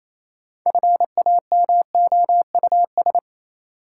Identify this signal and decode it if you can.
Morse code, heard as a keyed tone, about 28 wpm, 700 Hz. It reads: FAMOUS
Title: Sentences from Top 1500 Words - New Word Drill Alt - 28wpm